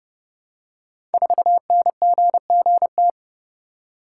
Morse code, heard as a keyed tone, about 29 wpm, 700 Hz. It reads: 4NGGT
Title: Code Groups - 5 Letter:Numbers - 30wpm